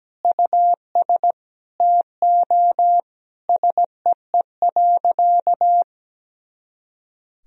Morse code, heard as a keyed tone, about 17 wpm, 700 Hz. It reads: US TO SEE.
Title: Sentences from Top 500 Words and Abbreviations 17wpm